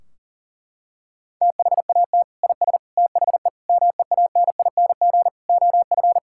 Morse code, heard as a keyed tone, about 37 wpm, 700 Hz. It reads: THAT IS THE MEANING OF
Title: Sentences from Top 1600 Words 40wpm